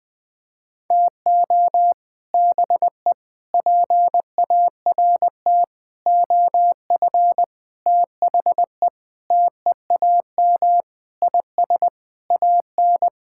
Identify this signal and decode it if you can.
Morse code, heard as a keyed tone, about 20 wpm, 700 Hz. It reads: TO BE PART OF THE TEAM IS AN
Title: Sentences with 5-Word Phrases - Spoken Beginning - 20wpm